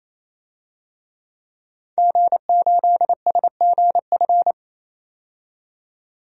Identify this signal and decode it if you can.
Morse code, heard as a keyed tone, about 28 wpm, 700 Hz. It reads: G8SGF
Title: DXCC-100 Call Signs - 5 Characters - 28wpm